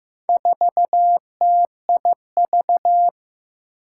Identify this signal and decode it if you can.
Morse code, heard as a keyed tone, about 15 wpm, 700 Hz. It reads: 4TIV
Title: Four Challenging Letter-Number Combinations 15wpm